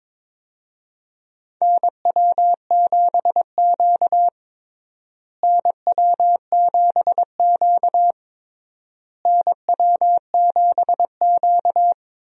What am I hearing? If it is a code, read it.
Morse code, heard as a keyed tone, about 22 wpm, 700 Hz. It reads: NW7Q NW7Q NW7Q
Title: USA Call Signs - 4 Characters - Repeated 3x - 22wpm